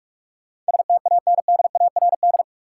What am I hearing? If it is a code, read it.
Morse code, heard as a keyed tone, about 41 wpm, 700 Hz. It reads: STANDARD
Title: Sentences from Top 1200 Words - New Word Drill Alt - 45wpm